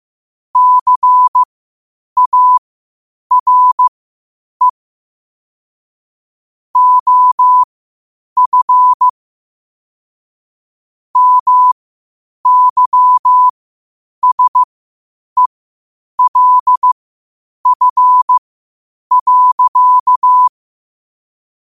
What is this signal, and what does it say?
Morse code, heard as a keyed tone, about 15 wpm, spaced as for 8 wpm, 1000 Hz. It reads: CARE OF MYSELF.
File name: qotdfiles / 8wpm